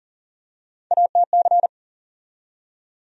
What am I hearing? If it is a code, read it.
Morse code, heard as a keyed tone, about 39 wpm, 700 Hz. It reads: ATC
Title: USA Call Sign Postfixes - 3 Letters 40wpm